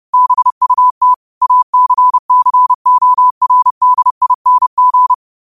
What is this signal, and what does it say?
Morse code, heard as a keyed tone, about 30 wpm, 1000 Hz. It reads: BUT ACCORDING
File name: qotdfiles / 30wpm